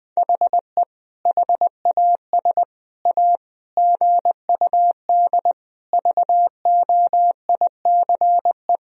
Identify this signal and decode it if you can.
Morse code, heard as a keyed tone, about 20 wpm, 700 Hz. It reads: HE HAS A GUD VOICE
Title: Sentences from Top 500 Words and Abbreviations 20wpm